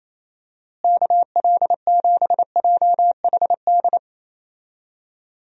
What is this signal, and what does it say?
Morse code, heard as a keyed tone, about 28 wpm, 700 Hz. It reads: KL7JHD
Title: USA Call Signs - 6 Characters - Repeated 3x - 28wpm